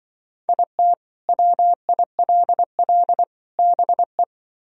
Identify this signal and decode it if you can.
Morse code, heard as a keyed tone, about 24 wpm, 700 Hz. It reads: IT WILL BE
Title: Speed-Racing - Sets of 3 Words 20wpm